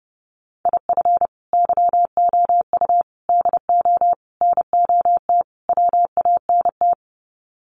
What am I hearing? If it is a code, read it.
Morse code, heard as a keyed tone, about 30 wpm, 700 Hz. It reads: IF YOU DO NOT WANT
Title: Sets of 5 Words 30wpm